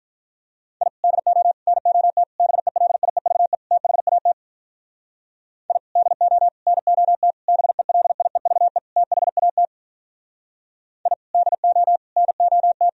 Morse code, heard as a keyed tone, about 44 wpm, 700 Hz. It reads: I DO NOT BELIEVE THAT I DO NOT BELIEVE THAT I DO NOT